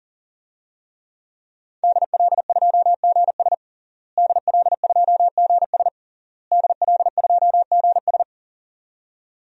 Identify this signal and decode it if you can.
Morse code, heard as a keyed tone, about 40 wpm, 700 Hz. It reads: DL2GS DL2GS DL2GS
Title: DXCC-100 Call Signs - 5 Characters - Repeated 3x - 40wpm